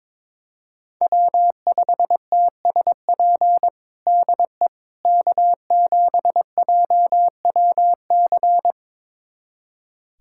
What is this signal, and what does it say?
Morse code, heard as a keyed tone, about 22 wpm, 700 Hz. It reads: W5TSP DE K7JWC
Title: USA Callsign DE Callsign - 5 Characters - 22wpm